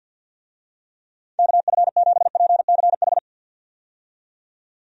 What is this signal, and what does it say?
Morse code, heard as a keyed tone, about 49 wpm, 700 Hz. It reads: KF7PCH
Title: USA Call Signs - 6 Characters - 50wpm